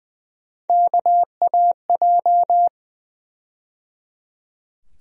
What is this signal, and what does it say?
Morse code, heard as a keyed tone, about 20 wpm, 700 Hz. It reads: KAJ
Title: USA Call Sign Postfixes - 3 Letters 20wpm